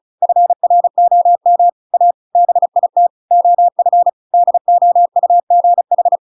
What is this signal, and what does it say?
Morse code, heard as a keyed tone, about 34 wpm, 700 Hz. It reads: FROM A BIT OF DOUGH
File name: Pinocchio / Pinocchio ch8 35 35 ik1hge